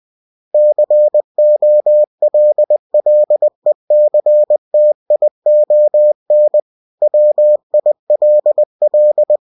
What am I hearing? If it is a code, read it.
Morse code, heard as a keyed tone, about 20 wpm, 600 Hz. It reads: COLLECTION WILL
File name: cw-books / winnie-the-pooh / winnie-20wpm-page-0120